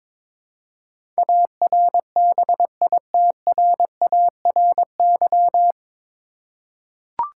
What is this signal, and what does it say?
Morse code, heard as a keyed tone, about 22 wpm, 700 Hz. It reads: ARBITRARY
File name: 9 Letter Words - First 3 Letters Given - 22wpm